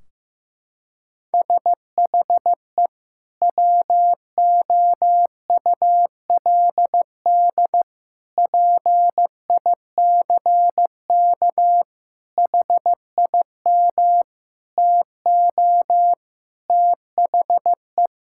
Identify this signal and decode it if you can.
Morse code, heard as a keyed tone, about 15 wpm, 700 Hz. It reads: SHE WOULD PICK HIM TO THE
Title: Sentences from Top 1600 Words 15wpm